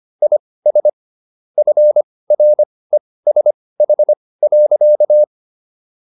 Morse code, heard as a keyed tone, about 25 wpm, spaced as for 18 wpm, 600 Hz. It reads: IS FRESH.